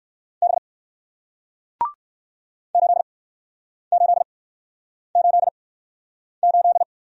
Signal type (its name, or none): Morse code